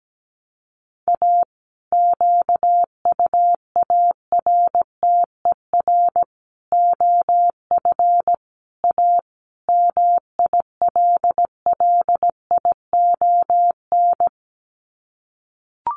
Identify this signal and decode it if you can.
Morse code, heard as a keyed tone, about 17 wpm, 700 Hz. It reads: A QUARTER OF A MILLION
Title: Sets of 5 Words Encore2 17wpm